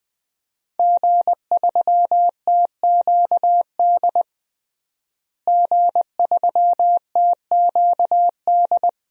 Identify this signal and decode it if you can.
Morse code, heard as a keyed tone, about 21 wpm, 700 Hz. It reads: G3TQD G3TQD
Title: DXCC-100 Call Signs - 5 Characters - Repeated 3x - 20wpm